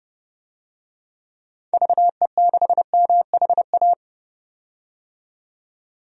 Morse code, heard as a keyed tone, about 32 wpm, 700 Hz. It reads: VE6MHA
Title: DXCC-100 Call Signs - 6 Characters - 30wpm